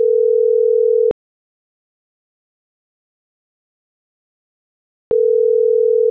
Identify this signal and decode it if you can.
Ringback tone (North America), 440 + 480 Hz.